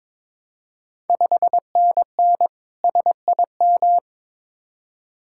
Morse code, heard as a keyed tone, about 21 wpm, 690 Hz. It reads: 5NN SIM
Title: Ontario QSO Party Exchange 22wpm